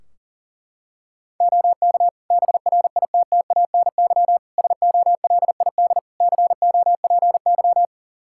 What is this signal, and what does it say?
Morse code, heard as a keyed tone, about 40 wpm, 700 Hz. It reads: OK BRITTANY SOLID COPY
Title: K1USN SST QSO Practice 40wpm